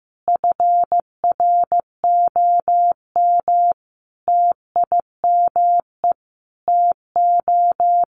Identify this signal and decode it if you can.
Morse code, heard as a keyed tone, about 15 wpm, 700 Hz. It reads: FROM TIME TO
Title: Sentences from Top 1300 Words 15wpm